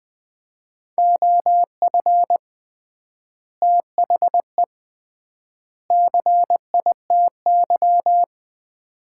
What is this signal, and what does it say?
Morse code, heard as a keyed tone, about 20 wpm, 700 Hz. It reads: OF THE CITY
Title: 3x Word-Spacing - Sets of 3 Words - 20wpm